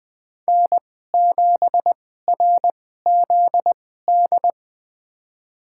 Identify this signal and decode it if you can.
Morse code, heard as a keyed tone, about 20 wpm, spaced as for 14 wpm, 700 Hz. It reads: N7RZD
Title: USA Call Signs - 5 Characters - 2x Character Spacing - 20wpm